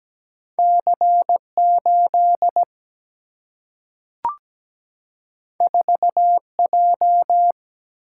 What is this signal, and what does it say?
Morse code, heard as a keyed tone, about 17 wpm, 700 Hz. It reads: C 8 4 J
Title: Single-Letter Single-Number 17wpm